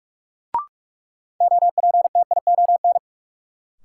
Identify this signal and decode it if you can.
Morse code, heard as a keyed tone, about 46 wpm, 700 Hz. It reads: OPTION
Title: Sentences from Top 1300 Words - New Word Drill Alt - 45wpm